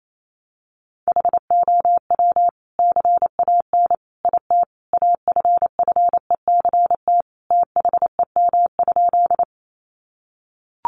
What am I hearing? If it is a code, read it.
Morse code, heard as a keyed tone, about 28 wpm, 700 Hz. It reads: HOW CAN IT AFFECT THEM?
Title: Sentences from Top 900 Words 28wpm